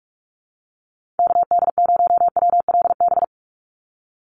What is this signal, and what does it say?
Morse code, heard as a keyed tone, about 46 wpm, 690 Hz. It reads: KD0WLB
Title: USA Call Signs 45wpm